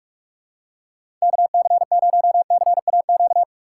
Morse code, heard as a keyed tone, about 44 wpm, 700 Hz. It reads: KC0CAQ
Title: USA Call Signs - 6 Characters - 6x Character Spacing - 45wpm